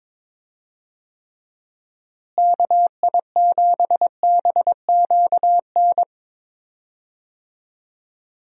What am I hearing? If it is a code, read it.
Morse code, heard as a keyed tone, about 21 wpm, 700 Hz. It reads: KI7BQN
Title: USA Call Signs - 6 Characters - Repeated 3x - 22wpm